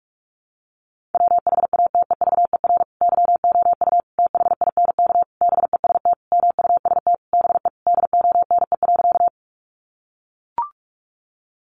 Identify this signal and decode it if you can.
Morse code, heard as a keyed tone, about 44 wpm, 700 Hz. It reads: WHATEVER YOU THINK BEST MUST BE DONE.
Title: Sentences from Top 1100 Words 45wpm